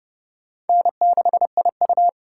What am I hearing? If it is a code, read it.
Morse code, heard as a keyed tone, about 35 wpm, 700 Hz. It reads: N6IU
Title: Speed Racing - USA 1x2 Call Signs 30wpm